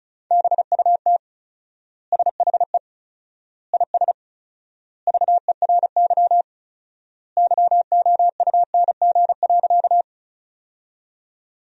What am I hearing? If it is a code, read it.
Morse code, heard as a keyed tone, about 36 wpm, 700 Hz. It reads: BUT SHE IS VERY YOUNG.
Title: 4x Word-Spacing - Sentences from Top 400 Words - 35wpm